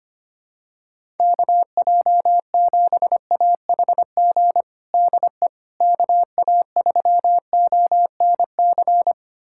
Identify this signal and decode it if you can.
Morse code, heard as a keyed tone, about 25 wpm, 700 Hz. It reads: KJ7AHG DE KA3ONC
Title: USA Callsign DE Callsign 25wpm